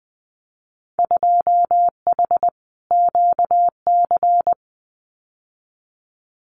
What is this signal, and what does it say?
Morse code, heard as a keyed tone, about 20 wpm, 700 Hz. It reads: 2H QC
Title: Winter Field Day Exchanges 20wpm